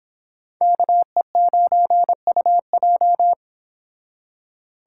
Morse code, heard as a keyed tone, about 26 wpm, 700 Hz. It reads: KE9UJ